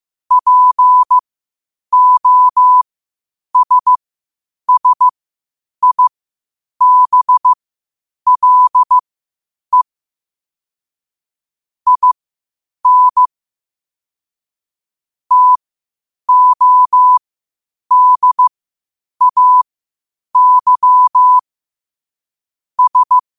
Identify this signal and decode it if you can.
Morse code, heard as a keyed tone, about 15 wpm, spaced as for 8 wpm, 1000 Hz. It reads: POSSIBLE IN TODAY S
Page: Quotes for Fri, 15 Aug 2025 in Morse Code at 8 words per minute.